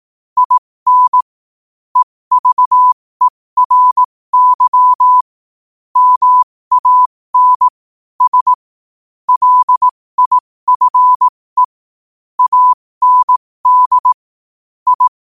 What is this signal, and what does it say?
Morse code, heard as a keyed tone, about 18 wpm, spaced as for 15 wpm, 1000 Hz. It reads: IN EVERY MAN S LIFE AND I
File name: qotdfiles / 15wpm